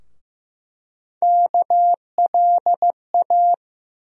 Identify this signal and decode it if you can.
Morse code, heard as a keyed tone, about 15 wpm, 700 Hz. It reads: KLA
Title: USA Call Sign Postfixes - 3 Letters 15wpm